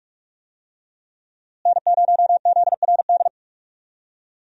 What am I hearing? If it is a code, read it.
Morse code, heard as a keyed tone, about 46 wpm, 700 Hz. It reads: N0ZRD
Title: USA Call Signs - 5 Characters - Repeated 2x - 45wpm